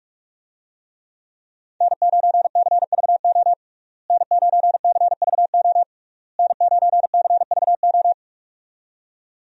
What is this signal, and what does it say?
Morse code, heard as a keyed tone, about 46 wpm, 700 Hz. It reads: N9CVO N9CVO N9CVO
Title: USA Call Signs - 5 Characters - Repeated 3x - 45wpm